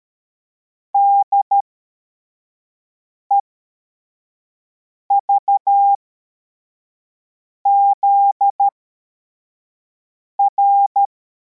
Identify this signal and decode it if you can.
Morse code, heard as a keyed tone, about 13 wpm, spaced as for 4 wpm, 800 Hz. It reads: DEVZR